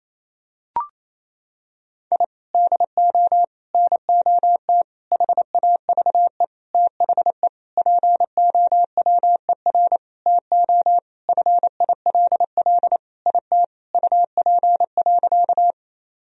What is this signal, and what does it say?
Morse code, heard as a keyed tone, about 28 wpm, 700 Hz. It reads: I DO NOT HAVE THE POWER TO FILL IT UP.
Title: Sentences from Top 1000 Words - Review Part 3 - 28wpm